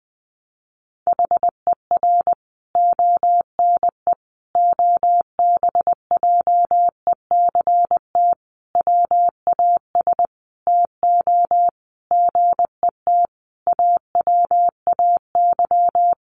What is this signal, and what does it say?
Morse code, heard as a keyed tone, about 20 wpm, 700 Hz. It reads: HER ONE OBJECT WAS TO GET AWAY
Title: Sentences from Top 1500 Words 20wpm